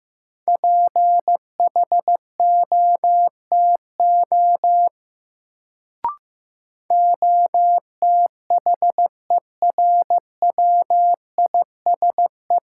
Morse code, heard as a keyed tone, about 15 wpm, 700 Hz. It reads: PHOTO OTHERWISE
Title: Sentences from Top 1200 Words - New Word Drill - 15wpm